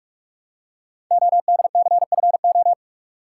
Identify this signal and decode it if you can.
Morse code, heard as a keyed tone, about 44 wpm, 700 Hz. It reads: ODCFO
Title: Code Groups - 5 Letters - 45wpm